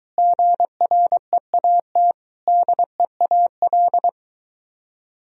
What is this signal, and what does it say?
Morse code, heard as a keyed tone, about 23 wpm, 700 Hz. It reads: GREAT DEAL